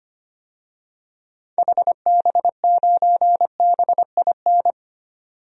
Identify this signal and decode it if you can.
Morse code, heard as a keyed tone, about 25 wpm, 700 Hz. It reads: HB9BIN
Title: DXCC-100 Call Signs - 6 Characters - 25wpm